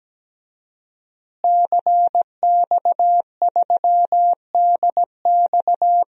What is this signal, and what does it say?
Morse code, heard as a keyed tone, about 17 wpm, 700 Hz. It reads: CX3DX
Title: DXCC-100 Call Signs - 5 Characters - Repeated 3x - 17wpm